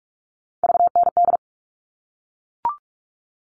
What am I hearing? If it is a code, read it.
Morse code, heard as a keyed tone, about 44 wpm, 700 Hz. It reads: VND
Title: All Cut Numbers - 3-Digit 45wpm